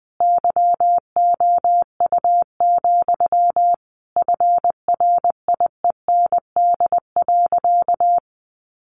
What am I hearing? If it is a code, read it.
Morse code, heard as a keyed tone, about 20 wpm, 700 Hz. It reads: YOU, FRIEND.
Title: Sentences from Top 1500 Words 20wpm